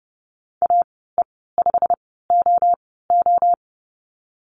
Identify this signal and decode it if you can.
Morse code, heard as a keyed tone, about 30 wpm, spaced as for 17 wpm, 700 Hz. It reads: AE5OO
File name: course / 30:10wpm / 116 Lesson 19g - Letter w - Call Signs 30wpm